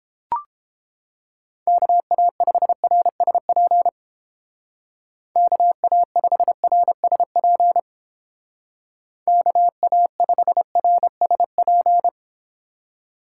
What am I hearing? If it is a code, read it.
Morse code, heard as a keyed tone, about 30 wpm, 700 Hz. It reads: KA5RSP KA5RSP KA5RSP